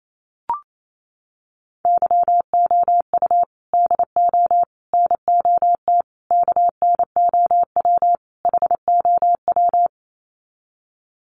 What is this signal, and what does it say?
Morse code, heard as a keyed tone, about 28 wpm, 700 Hz. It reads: YOU DO NOT KNOW HOW
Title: Sets of 5 Words Encore 28wpm